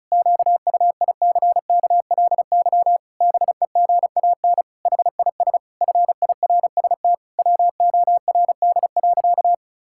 Morse code, heard as a keyed tone, about 34 wpm, 700 Hz. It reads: QUICKLY BEGAN HIS FIRST WORD.
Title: Pangram Challenge v2 - 35wpm